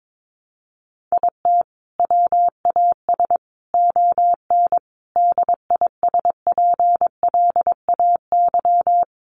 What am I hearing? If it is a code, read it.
Morse code, heard as a keyed tone, about 22 wpm, 700 Hz. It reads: IT WAS ON DISPLAY
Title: Sentences from Top 1200 Words 22wpm